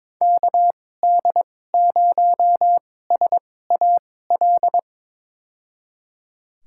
Morse code, heard as a keyed tone, about 22 wpm, spaced as for 16 wpm, 700 Hz. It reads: KD0SAL
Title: USA Call Signs - 6 Characters - 2x Character Spacing - 22wpm